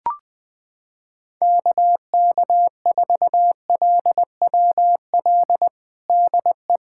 Morse code, heard as a keyed tone, about 20 wpm, 700 Hz. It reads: KK4LWL DE
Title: USA Callsign DE Callsign - 6 Characters - 20wpm